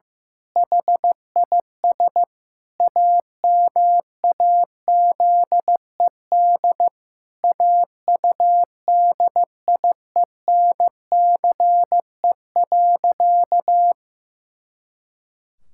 Morse code, heard as a keyed tone, about 15 wpm, 700 Hz. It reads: HIS AMAZED AUDIENCE.
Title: Pangram Challenge v2 - 15wpm